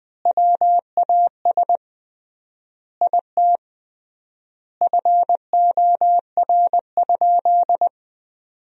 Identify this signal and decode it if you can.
Morse code, heard as a keyed tone, about 20 wpm, 700 Hz. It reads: WAS IT FOR?
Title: 3x Word-Spacing - Sentences from Top 100 Words - 20wpm